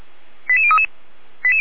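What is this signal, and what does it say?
Five-tone selective call (ZVEI), heard as quick runs of five tones, tones of about 70 ms. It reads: 88033
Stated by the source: ZVEI five-tone call received by SDR, logged by Feuerwehr Orlamünde (fire brigade).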